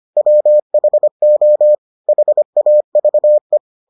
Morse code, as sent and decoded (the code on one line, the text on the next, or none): .-- .... --- / .... .- ...- .
WHO HAVE